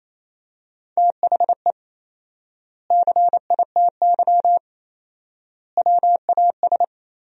Morse code, heard as a keyed tone, about 28 wpm, 700 Hz. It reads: THE CITY WAS
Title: 4x Word-Spacing - Sentences from Top 500 Words - 28wpm